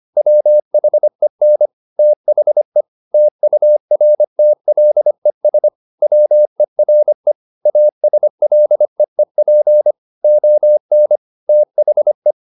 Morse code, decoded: WHEN THE TURTLES WERE ASLEEP ON THE

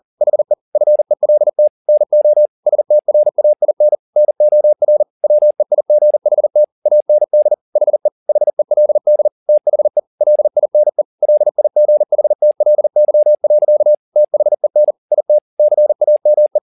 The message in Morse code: HE FELT NO STRAIN NOR WEIGHT AND HE HELD THE LINE LIGHTLY. THEN IT CAME

